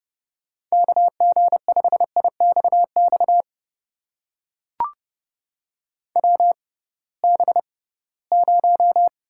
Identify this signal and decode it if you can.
Morse code, heard as a keyed tone, about 30 wpm, spaced as for 19 wpm, 700 Hz. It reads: KG5IXX WB0
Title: USA Call Signs - 6 Characters - 6x Character Spacing - 30wpm